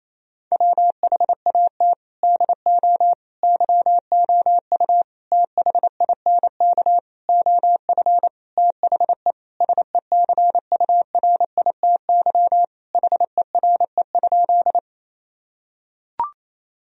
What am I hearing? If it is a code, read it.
Morse code, heard as a keyed tone, about 28 wpm, 700 Hz. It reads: WHAT DO YOU THINK OF THE SECURITY HERE?
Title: Sentences from Top 1000 Words - Review Part 4 - 28wpm